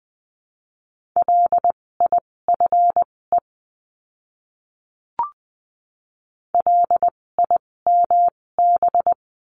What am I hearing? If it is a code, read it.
Morse code, heard as a keyed tone, about 20 wpm, spaced as for 16 wpm, 700 Hz. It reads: LIFE LIMB
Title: 256 Lesson 42a - Words 12-20wpm